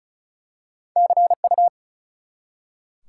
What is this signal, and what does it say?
Morse code, heard as a keyed tone, about 35 wpm, 700 Hz. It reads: CU
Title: Top 180 QSO Elements 35wpm